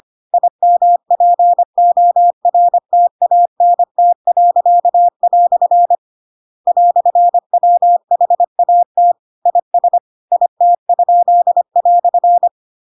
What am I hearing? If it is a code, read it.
Morse code, heard as a keyed tone, about 25 wpm, 700 Hz. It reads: IMPORTANT." "WHAT IS IT?"
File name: Pinocchio / Pinocchio ch8 25 25 ik1hge